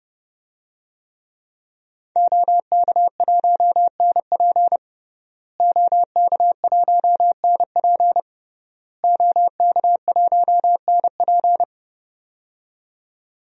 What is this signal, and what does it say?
Morse code, heard as a keyed tone, about 30 wpm, 700 Hz. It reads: OK1NP OK1NP OK1NP
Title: DXCC-100 Call Signs - 5 Characters - Repeated 3x - 30wpm